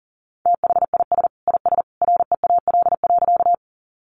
Morse code, heard as a keyed tone, about 40 wpm, 700 Hz. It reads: THIS IS REAL.
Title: Sentences from Top 300 Words 40wpm